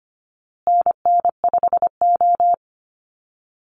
Morse code, heard as a keyed tone, about 25 wpm, 700 Hz. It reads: NN5O
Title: Call Signs Repeated 2x - 25wpm